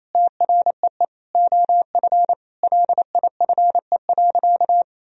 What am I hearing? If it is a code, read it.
Morse code, heard as a keyed tone, about 28 wpm, 700 Hz. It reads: TREE OF LIFE.
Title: Sentences from Top 1000 Words - Review Part 1 - 28wpm